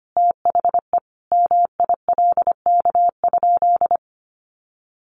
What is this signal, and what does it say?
Morse code, heard as a keyed tone, about 25 wpm, 700 Hz. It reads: THE MILK?
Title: Sentences from Top 1500 Words 25wpm